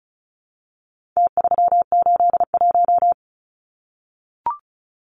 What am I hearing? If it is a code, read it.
Morse code, heard as a keyed tone, about 34 wpm, 700 Hz. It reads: T381
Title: Numbers - Contest Style - 4-Digit 35wpm